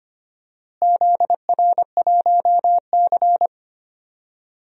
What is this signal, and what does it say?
Morse code, heard as a keyed tone, about 25 wpm, 700 Hz. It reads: ZR1C
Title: DXCC-100 Call Signs - 4 Characters - Repeated 3x - 25wpm